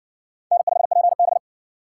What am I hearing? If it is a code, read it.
Morse code, heard as a keyed tone, about 57 wpm, 700 Hz. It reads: N5PB